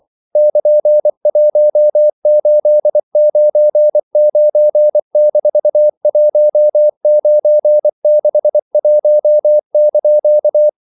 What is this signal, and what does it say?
Morse code, heard as a keyed tone, about 24 wpm, 600 Hz. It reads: <KN>1899-1961)
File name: CW / Practice / EnglishQuotations0424